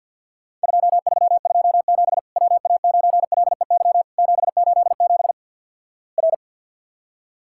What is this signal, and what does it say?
Morse code, heard as a keyed tone, about 50 wpm, 700 Hz. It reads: 2327 WA9LEY 787 R
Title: Worked All Europe DX Contest - QTC Exchange - 50wpm